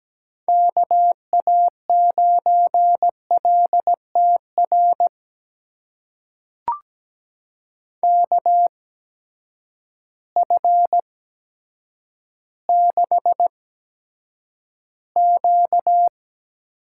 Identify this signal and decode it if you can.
Morse code, heard as a keyed tone, about 17 wpm, spaced as for 8 wpm, 700 Hz. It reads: KA9LTR KF6Q
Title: USA Call Signs - 6 Characters - 8x Character Spacing - 17wpm